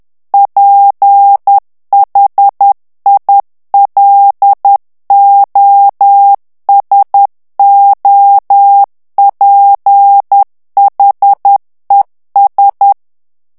Morse code, heard as a keyed tone, about 10 wpm, 800 Hz. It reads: PHILOSOPHES
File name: morse mp3 / FRANCAIS TEXTE CLAIR 2 VITESSE 10 MOTS MINUTES